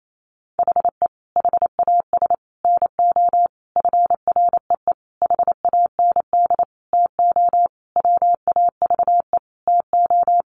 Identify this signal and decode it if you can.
Morse code, heard as a keyed tone, about 28 wpm, 700 Hz. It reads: HE HAS NO FREE HAND TO WAVE TO
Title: Sentences from Top 1400 Words 28wpm